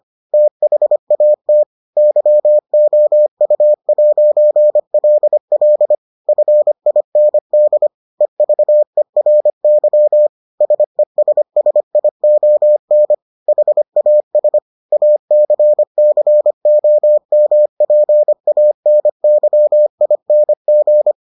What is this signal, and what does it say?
Morse code, heard as a keyed tone, about 25 wpm, 600 Hz. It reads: THAT YOU'LL FIND EVERY SESSION HAS ACCOMPANYING